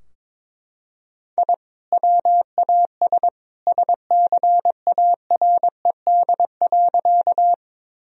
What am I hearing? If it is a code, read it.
Morse code, heard as a keyed tone, about 21 wpm, 700 Hz. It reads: I WAS SCARED.
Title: Sentences from Top 1000 Words - Review Part 4 - 22wpm